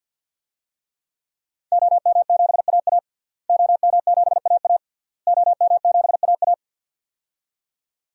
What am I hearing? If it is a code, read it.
Morse code, heard as a keyed tone, about 52 wpm, 700 Hz. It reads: OM7AA OM7AA OM7AA
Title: DXCC-100 Call Signs - 5 Characters - Repeated 3x - 50wpm